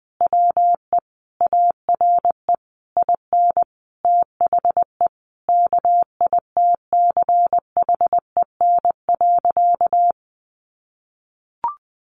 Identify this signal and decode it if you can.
Morse code, heard as a keyed tone, about 20 wpm, 700 Hz. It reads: WE ARE IN THE KITCHEN.
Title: Sentences from Top 1400 Words 20wpm